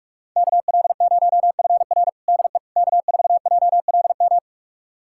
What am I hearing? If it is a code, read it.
Morse code, heard as a keyed tone, about 46 wpm, 700 Hz. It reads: KL0FR DE K4JLM
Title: USA Callsign DE Callsign - 5 Characters - 45wpm